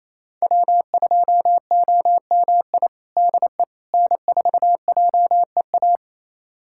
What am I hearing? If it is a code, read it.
Morse code, heard as a keyed tone, about 28 wpm, 700 Hz. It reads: W2OMI DE N4JEA
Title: USA Callsign DE Callsign - 5 Characters - 28wpm